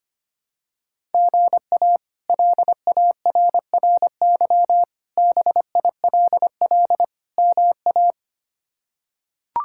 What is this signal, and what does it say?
Morse code, heard as a keyed tone, about 25 wpm, 700 Hz. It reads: GA LARRY BILL MA
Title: K1USN SST QSO Practice - 25wpm